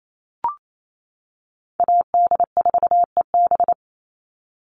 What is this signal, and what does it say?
Morse code, heard as a keyed tone, about 28 wpm, 700 Hz. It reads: AD4EB
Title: Call Signs - 28wpm